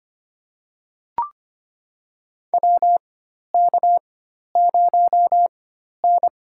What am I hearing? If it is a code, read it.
Morse code, heard as a keyed tone, about 25 wpm, spaced as for 12 wpm, 700 Hz. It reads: WK0N
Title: USA Call Signs - 4 Characters - 4x Character Spacing - 25wpm